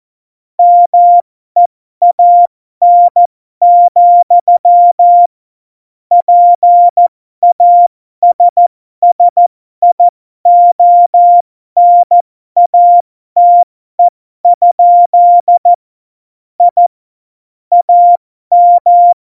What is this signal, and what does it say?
Morse code, heard as a keyed tone, about 14 wpm, spaced as for 12 wpm, 700 Hz. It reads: MEAN, PASSIONATE? I AM
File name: Morsekurs / Witze